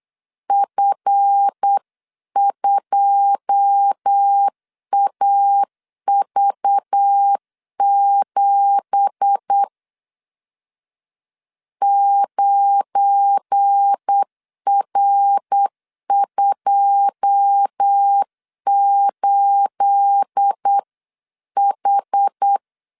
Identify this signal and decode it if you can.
Morse code, heard as a keyed tone, about 8 wpm, 800 Hz. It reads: F2AV7 9R28H
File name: CW / DET / 1 9 1500C S B 8wpm 5624